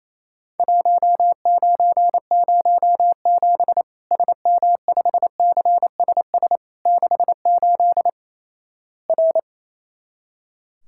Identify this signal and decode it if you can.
Morse code, heard as a keyed tone, about 28 wpm, 700 Hz. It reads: 1907 SM5CSS 68 R
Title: Worked All Europe DX Contest - QTC Exchange - 28wpm